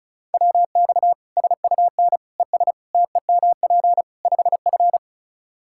Morse code, heard as a keyed tone, about 34 wpm, 700 Hz. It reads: WX SUN ES TEMP 5F
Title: Practice QSO - Exchange 2 - Info - 35wpm